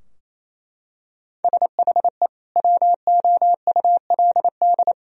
Morse code, heard as a keyed tone, about 28 wpm, 700 Hz. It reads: SHE WOULD
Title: Sentences from Top 1000 Words - Review Part 2 - 28wpm